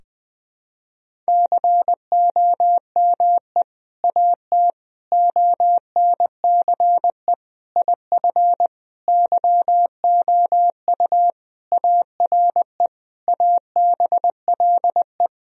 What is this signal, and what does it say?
Morse code, heard as a keyed tone, about 20 wpm, 700 Hz. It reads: COME AT ONCE IF YOU ARE ABLE
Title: Sentences from Top 1000 Words - Review Part 2 - 20wpm